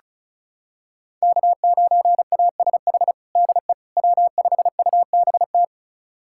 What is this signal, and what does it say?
Morse code, heard as a keyed tone, about 36 wpm, 690 Hz. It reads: K9ASH DE W5UBT
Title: USA Callsign DE Callsign - 5 Characters - 35wpm